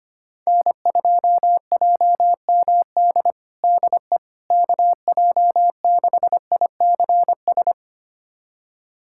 Morse code, heard as a keyed tone, about 25 wpm, 700 Hz. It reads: N2JMD DE KJ6ICS
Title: USA Callsign DE Callsign 25wpm